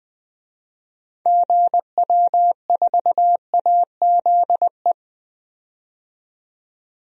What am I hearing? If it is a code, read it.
Morse code, heard as a keyed tone, about 20 wpm, 700 Hz. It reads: GW4AZE
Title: DXCC-100 Call Signs - 6 Characters - 20wpm